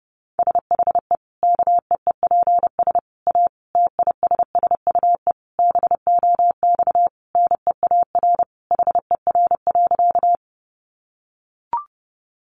Morse code, heard as a keyed tone, about 30 wpm, 700 Hz. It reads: SHE KEEPS A TISSUE BOX NEAR HER.
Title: Easy Phrases 30wpm